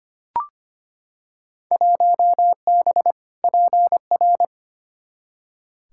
Morse code, heard as a keyed tone, about 25 wpm, 700 Hz. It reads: 1B PR
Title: Field Day Exchanges Encore - 25wpm